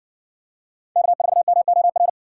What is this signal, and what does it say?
Morse code, heard as a keyed tone, about 58 wpm, 700 Hz. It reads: K4MOR